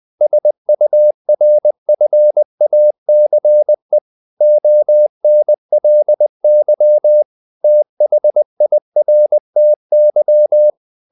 Morse code, heard as a keyed tone, about 20 wpm, 600 Hz. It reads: SURFACE ONLY THIRTY
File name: the old man and the sea part 2 20wpm